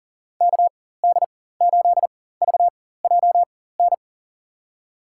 Morse code, heard as a keyed tone, about 37 wpm, spaced as for 18 wpm, 700 Hz. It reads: KD8VJN
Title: USA Call Signs - 6 Characters - 4x Character Spacing - 40wpm